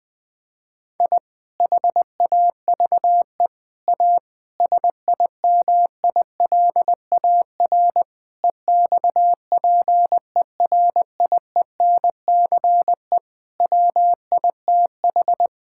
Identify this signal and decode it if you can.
Morse code, heard as a keyed tone, about 20 wpm, 700 Hz. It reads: I HAVE A SIMILAR EXPERIENCE WITH
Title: Sentences from Top 1000 Words - Review Part 5 - 20wpm